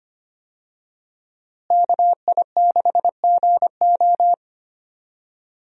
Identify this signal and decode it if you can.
Morse code, heard as a keyed tone, about 25 wpm, 700 Hz. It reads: KI6GO
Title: USA Call Signs - 5 Characters - Repeated 2x - 25wpm